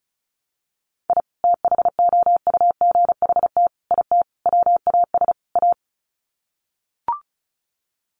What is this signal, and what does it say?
Morse code, heard as a keyed tone, about 34 wpm, 700 Hz. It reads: I THOUGHT IT WAS A
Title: Sets of 5 Words 35wpm